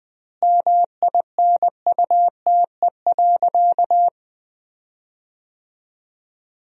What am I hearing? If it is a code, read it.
Morse code, heard as a keyed tone, about 21 wpm, 700 Hz. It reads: MINUTE.
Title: 4x Word-Spacing - Sentences from Top 500 Words - 20wpm